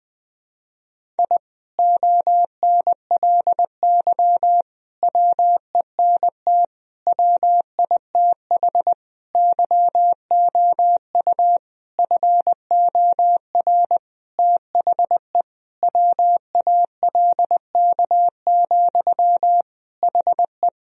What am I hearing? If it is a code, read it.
Morse code, heard as a keyed tone, about 20 wpm, 700 Hz. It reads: I ONLY WENT WITH YOU FOR THE WALK, HE
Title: Sentences from Top 1000 Words - Review Part 1 - 20wpm